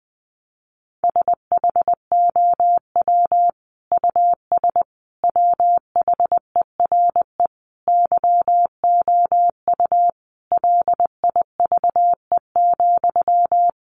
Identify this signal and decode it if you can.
Morse code, heard as a keyed tone, about 20 wpm, 700 Hz. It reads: SHOW US WHERE YOU LIVE,
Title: Sentences from Top 200 Words 20wpm